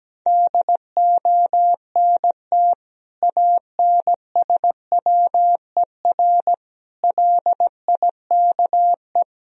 Morse code, heard as a keyed tone, about 17 wpm, 700 Hz. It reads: DONT ANSWER LIKE
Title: Sentences from Top 400 Words and Abbreviations 17wpm